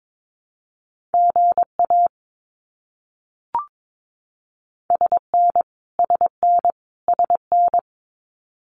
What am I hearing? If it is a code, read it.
Morse code, heard as a keyed tone, about 22 wpm, 700 Hz. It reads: GA SN SN SN
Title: Abbreviations Repeated 3x - 22wpm